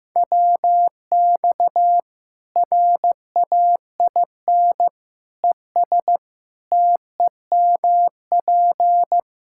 Morse code, heard as a keyed tone, about 16 wpm, 700 Hz. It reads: WX RAIN ES TEMP
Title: Practice QSO - Exchange 2 - Info - 15wpm